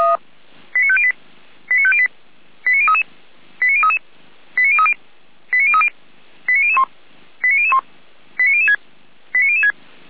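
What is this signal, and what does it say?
Five-tone selective call (ZVEI), heard as quick runs of five tones, tones of about 70 ms. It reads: 89508 89508 89033 89033 89030 89030 89001 89001 89006 89006
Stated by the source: Five-tone ZVEI selective call 89014, received by SDR.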